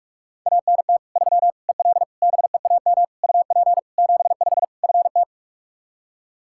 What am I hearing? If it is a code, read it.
Morse code, heard as a keyed tone, about 46 wpm, 700 Hz. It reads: ANT 3 EL BEAM UP 75 FT
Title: Practice QSO - Exchange 2 - Info - 45wpm